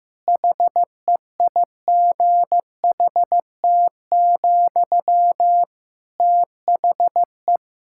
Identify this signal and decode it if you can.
Morse code, heard as a keyed tone, about 16 wpm, 700 Hz. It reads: HEIGHT, THE
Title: Sentences from Top 1600 Words 15wpm